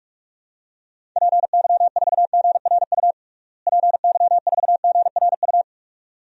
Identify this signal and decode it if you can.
Morse code, heard as a keyed tone, about 41 wpm, 700 Hz. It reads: PY4GRU PY4GRU
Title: DXCC-100 Call Signs - 6 Characters - Repeated 2x - 45wpm